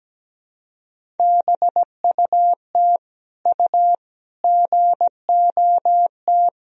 Morse code, heard as a keyed tone, about 17 wpm, 700 Hz. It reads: BUT U GOT
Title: Sentences from Top 300 Words and Abbreviations 17wpm